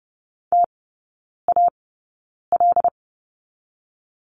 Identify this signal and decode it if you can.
Morse code, heard as a keyed tone, about 30 wpm, spaced as for 9 wpm, 690 Hz. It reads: TAL
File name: Lesson 17e - Letter u - Three Characters 30wpm